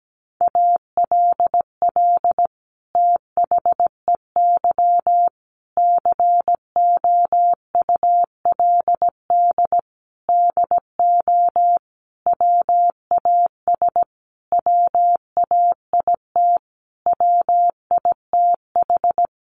Morse code, heard as a keyed tone, about 17 wpm, 700 Hz. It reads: ALL THEY COULD DO WAS WAIT WITH
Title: Sentences from Top 1300 Words 17wpm